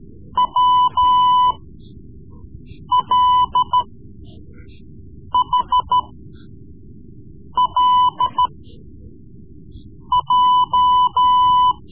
Morse code, heard as a keyed tone, about 13 wpm, spaced as for 5 wpm, 1000 Hz. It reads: WLHLJ
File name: Morse Lessons / Morse Lessons / Tape-3B